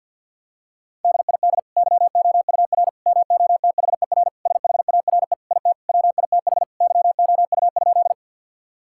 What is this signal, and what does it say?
Morse code, heard as a keyed tone, about 51 wpm, 700 Hz. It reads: DID YOUR MOTHER SHARE IT WITH YOU?
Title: Sentences from Top 1000 Words - Review Part 6 - 50wpm